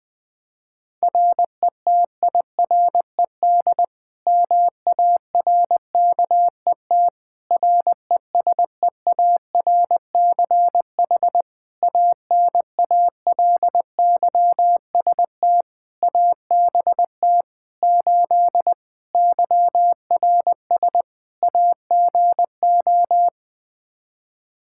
Morse code, heard as a keyed tone, about 21 wpm, 700 Hz. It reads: RETIRED MARKET RESEARCH ANALYST ABT 8 YRS AGO
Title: Practice QSO - Exchange 3 - More Info - 20wpm